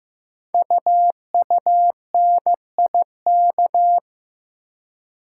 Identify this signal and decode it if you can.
Morse code, heard as a keyed tone, about 15 wpm, 690 Hz. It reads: UUNIK
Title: Code Groups - 5 Letters - 15wpm